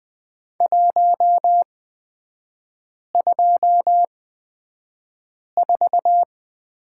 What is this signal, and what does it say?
Morse code, heard as a keyed tone, about 20 wpm, spaced as for 5 wpm, 700 Hz. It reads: 124
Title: 224 Lesson 36e - BT - Three Characters 20wpm